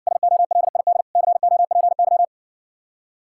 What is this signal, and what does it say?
Morse code, heard as a keyed tone, about 61 wpm, 700 Hz. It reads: SOLID COPY